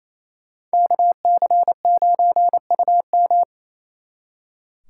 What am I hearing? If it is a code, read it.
Morse code, heard as a keyed tone, about 35 wpm, 700 Hz. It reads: KC9UM